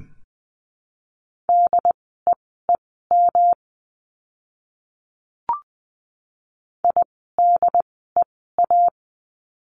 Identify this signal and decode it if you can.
Morse code, heard as a keyed tone, about 20 wpm, spaced as for 14 wpm, 700 Hz. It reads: DEEM IDEA
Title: 108 Lesson 18f - Letter m - Words 20wpm